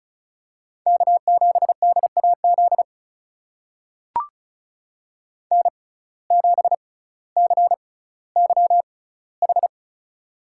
Morse code, heard as a keyed tone, about 36 wpm, spaced as for 18 wpm, 700 Hz. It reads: K7DAZ N7CYH
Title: USA Call Signs - 5 Characters - 6x Character Spacing - 35wpm